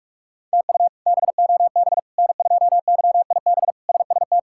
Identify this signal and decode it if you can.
Morse code, heard as a keyed tone, about 46 wpm, 700 Hz. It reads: TU BOB N2YIB SST
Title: K1USN SST QSO Practice - 45wpm